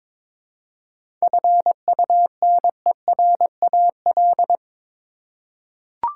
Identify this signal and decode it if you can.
Morse code, heard as a keyed tone, about 22 wpm, 700 Hz. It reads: FUNERAL
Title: Seven Letter Words - 22wpm